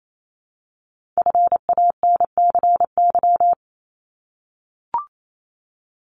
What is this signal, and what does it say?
Morse code, heard as a keyed tone, about 28 wpm, 700 Hz. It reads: FANCY
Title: Multi-letter Phonemes 28wpm